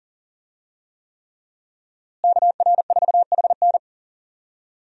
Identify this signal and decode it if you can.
Morse code, heard as a keyed tone, about 41 wpm, 700 Hz. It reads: KR4HN
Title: USA Call Signs - 5 Characters - Repeated 3x - 40wpm